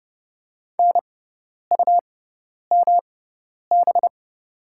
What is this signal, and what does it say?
Morse code, heard as a keyed tone, about 31 wpm, spaced as for 10 wpm, 700 Hz. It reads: NUMB
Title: 6x Character Spacing - Top 100 Words - 30wpm